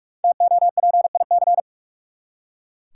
Morse code, heard as a keyed tone, about 43 wpm, 700 Hz. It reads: TOPIC
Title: Five Letter Words - 45wpm